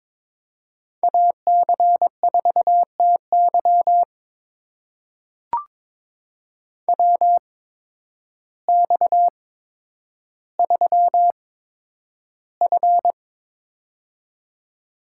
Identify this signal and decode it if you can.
Morse code, heard as a keyed tone, about 21 wpm, spaced as for 10 wpm, 700 Hz. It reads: AC4TY WX3F
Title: USA Call Signs - 8x Character Spacing - 22wpm